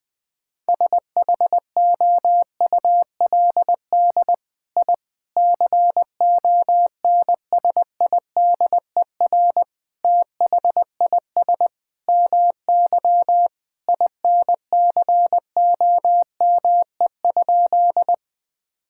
Morse code, heard as a keyed tone, about 20 wpm, 700 Hz. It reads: SHOULD I CONSIDER THIS MY INCOME?
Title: Sentences from Top 1000 Words - Review Part 5 - 20wpm